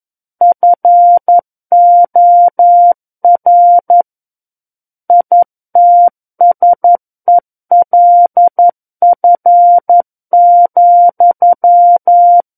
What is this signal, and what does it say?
Morse code, heard as a keyed tone, about 11 wpm, 700 Hz. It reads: FOR ITSELF,
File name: 02-Chapter 2 - 11 WPM